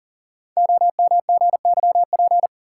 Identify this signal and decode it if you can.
Morse code, heard as a keyed tone, about 42 wpm, 700 Hz. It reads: OMGYP
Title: Code Groups - 5 Letters - 40wpm